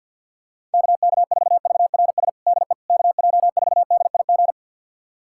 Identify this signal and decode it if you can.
Morse code, heard as a keyed tone, about 50 wpm, 700 Hz. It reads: KK4VRS DE KJ4DIG
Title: USA Callsign DE Callsign - 6 Characters - 50wpm